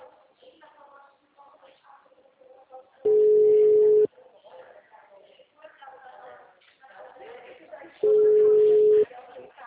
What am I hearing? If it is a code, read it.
Ringback tone (Europe, 425 Hz), with speech.